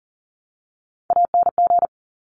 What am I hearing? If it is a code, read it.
Morse code, heard as a keyed tone, about 42 wpm, 700 Hz. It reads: ANG
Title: Top Three Letter Combinations in Words 40wpm